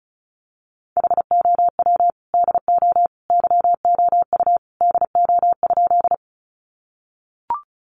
Morse code, heard as a keyed tone, about 34 wpm, 700 Hz. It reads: HOW DO YOU DO?
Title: Sentences from Top 100 Words 35wpm